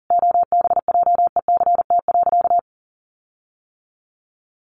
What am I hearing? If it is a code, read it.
Morse code, heard as a keyed tone, about 40 wpm, 700 Hz. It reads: OBJECT.
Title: Sentences from Top 1400 Words 40wpm